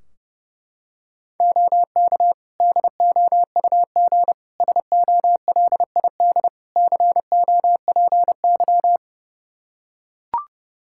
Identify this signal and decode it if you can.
Morse code, heard as a keyed tone, about 30 wpm, 700 Hz. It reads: OK DOUG SOLID COPY
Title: Practice QSO - Exchange 3 - More Info - 30wpm